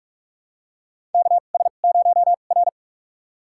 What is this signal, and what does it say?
Morse code, heard as a keyed tone, about 46 wpm, spaced as for 32 wpm, 700 Hz. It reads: KS0R
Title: USA Call Signs - 4 Characters - 2x Character Spacing - 45wpm